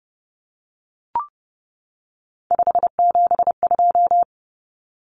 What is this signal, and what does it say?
Morse code, heard as a keyed tone, about 30 wpm, 700 Hz. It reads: 572
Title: Common Cut Numbers - 3-Digit 30wpm